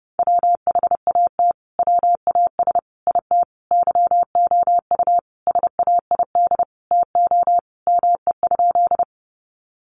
Morse code, heard as a keyed tone, about 28 wpm, 700 Hz. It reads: WHAT WAS IT YOU SAID TO ME?
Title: Sentences from Top 200 Words 30wpm